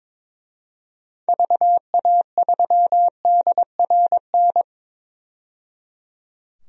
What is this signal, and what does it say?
Morse code, heard as a keyed tone, about 22 wpm, 700 Hz. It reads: VA3DRN
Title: DXCC-100 Call Signs - 6 Characters - 22wpm